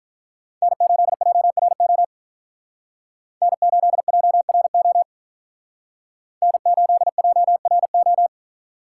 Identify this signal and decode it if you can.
Morse code, heard as a keyed tone, about 44 wpm, 700 Hz. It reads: N8JRO N8JRO N8JRO